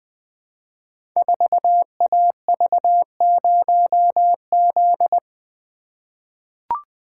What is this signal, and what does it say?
Morse code, heard as a keyed tone, about 20 wpm, 700 Hz. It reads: 4AV0Z
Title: Code Groups - 5 Letter:Numbers - 20wpm